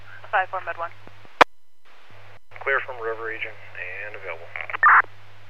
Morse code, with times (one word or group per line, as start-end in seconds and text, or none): none